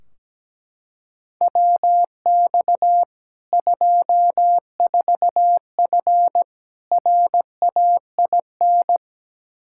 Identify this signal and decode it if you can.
Morse code, heard as a keyed tone, about 17 wpm, 700 Hz. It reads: WX 24F RAIN
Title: K1USN SST QSO Practice 17wpm